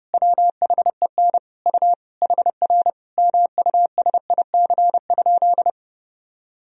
Morse code, heard as a keyed tone, about 32 wpm, 700 Hz. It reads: WHEN U HR MUSIC?
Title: Sentences from Top 300 Words and Abbreviations 30wpm